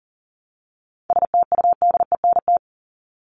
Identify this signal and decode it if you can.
Morse code, heard as a keyed tone, about 40 wpm, 700 Hz. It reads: STUDENT
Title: ES:S Suffix - Top 100 Words with Base Words 40wpm